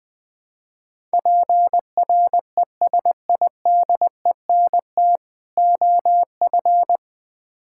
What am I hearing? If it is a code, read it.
Morse code, heard as a keyed tone, about 20 wpm, 700 Hz. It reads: PRESIDENT OF
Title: Build A Phrase - 5-Word Phrases - 20wpm